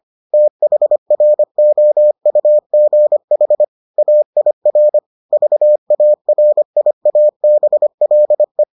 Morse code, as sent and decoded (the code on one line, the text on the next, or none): - .... .-. --- ..- --. .... / .- .. .-. / ...- .- .-. .. .- -... .-.. .
THROUGH AIR VARIABLE